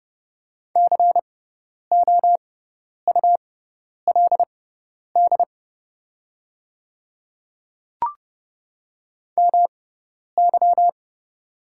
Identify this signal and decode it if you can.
Morse code, heard as a keyed tone, about 30 wpm, spaced as for 10 wpm, 700 Hz. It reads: COULD MY
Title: 6x Character Spacing - Top 300 Words - 30wpm